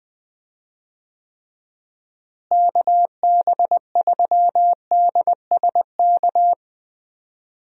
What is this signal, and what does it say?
Morse code, heard as a keyed tone, about 20 wpm, 700 Hz. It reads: KB3DSK
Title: USA Call Signs - 6 Characters - Repeated 2x - 20wpm